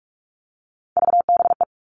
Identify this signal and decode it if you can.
Morse code, heard as a keyed tone, about 46 wpm, 700 Hz. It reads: VBE
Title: Three Challenging Letter-Number Combinations 45wpm